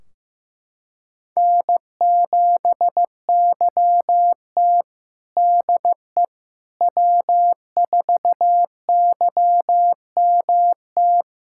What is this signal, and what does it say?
Morse code, heard as a keyed tone, about 16 wpm, 700 Hz. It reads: N7YT DE W4YMT
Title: K1USN SST QSO Practice 15wpm